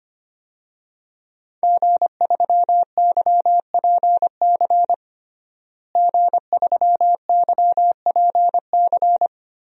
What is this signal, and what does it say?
Morse code, heard as a keyed tone, about 25 wpm, 700 Hz. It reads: G3YPC G3YPC
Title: DXCC-100 Call Signs - 5 Characters - Repeated 2x - 25wpm